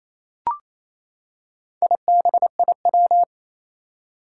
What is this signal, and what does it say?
Morse code, heard as a keyed tone, about 27 wpm, 700 Hz. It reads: IBIW
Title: Four Challenging Letter-Number Combinations 28wpm